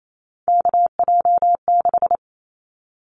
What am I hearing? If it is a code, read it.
Morse code, heard as a keyed tone, about 28 wpm, 700 Hz. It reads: KJ6
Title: Callsign Prefixes for Top 25 DXCC - Repeated 3x - 28wpm